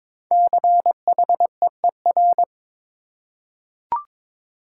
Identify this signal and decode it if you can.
Morse code, heard as a keyed tone, about 22 wpm, 700 Hz. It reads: CHEER
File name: 5 Letter Words - First 3 Letters Given - 22wpm